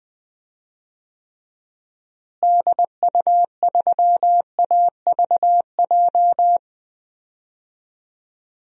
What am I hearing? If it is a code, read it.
Morse code, heard as a keyed tone, about 20 wpm, 700 Hz. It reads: DU3AVJ
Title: DXCC-100 Call Signs - 6 Characters - Repeated 2x - 20wpm